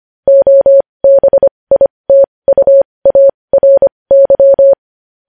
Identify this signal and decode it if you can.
Morse code, heard as a keyed tone, about 25 wpm, spaced as for 20 wpm, 550 Hz. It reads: OBITUARY
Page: News Headlines in Morse Code at 20 WPM